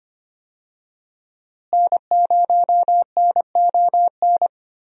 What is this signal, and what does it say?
Morse code, heard as a keyed tone, about 25 wpm, 700 Hz. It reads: N0NON
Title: USA Call Signs - 5 Characters - Repeated 3x - 25wpm